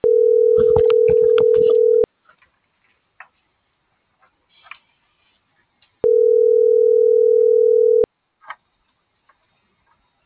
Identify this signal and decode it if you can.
Ringback tone (North America), 440 + 480 Hz.
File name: CENTRAL TERRESTRE / out cris.f